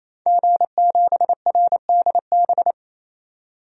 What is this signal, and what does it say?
Morse code, heard as a keyed tone, about 28 wpm, 700 Hz. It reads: G7RDB
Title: DXCC-100 Call Signs - 5 Characters - 28wpm